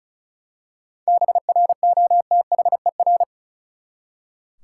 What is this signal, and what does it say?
Morse code, heard as a keyed tone, about 34 wpm, 700 Hz. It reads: BROTHER
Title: Seven Letter Words - 35wpm